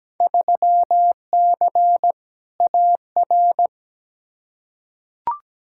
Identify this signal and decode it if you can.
Morse code, heard as a keyed tone, about 17 wpm, 700 Hz. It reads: 3C AR
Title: Field Day Exchanges Encore - 17wpm